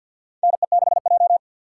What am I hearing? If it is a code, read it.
Morse code, heard as a keyed tone, about 53 wpm, 700 Hz. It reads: NE6J
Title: USA Call Signs - 6x Character Spacing - 50wpm